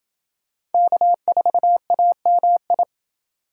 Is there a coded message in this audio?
Yes: Morse code from 0.7 to 2.8 s, heard as a keyed tone, about 29 wpm, 700 Hz. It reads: K4AMI